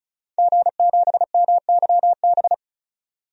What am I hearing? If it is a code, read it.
Morse code, heard as a keyed tone, about 34 wpm, 700 Hz. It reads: G7MYB
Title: DXCC-100 Call Signs - 5 Characters - 35wpm